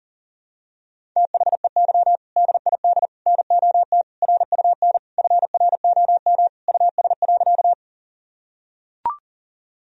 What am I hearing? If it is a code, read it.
Morse code, heard as a keyed tone, about 40 wpm, 700 Hz. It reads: THEY DID NOT RUN FROM US.
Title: Sentences from Top 1000 Words - Review Part 1 - 40wpm